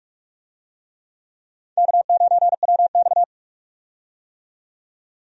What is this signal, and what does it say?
Morse code, heard as a keyed tone, about 46 wpm, 700 Hz. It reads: K9WX
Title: USA Call Signs - 4 Characters - 45wpm